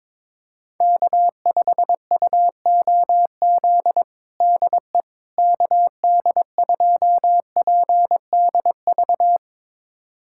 Morse code, heard as a keyed tone, about 22 wpm, 700 Hz. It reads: K5UOZ DE KD2PDV
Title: USA Callsign DE Callsign 22wpm